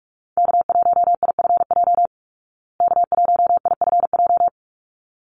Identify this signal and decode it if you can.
Morse code, heard as a keyed tone, about 44 wpm, 700 Hz. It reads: K1IFJ K1IFJ
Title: Call Signs Repeated 3x - 45wpm